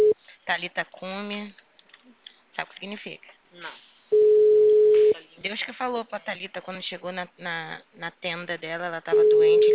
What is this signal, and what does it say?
Ringback tone (Europe, 425 Hz), with speech.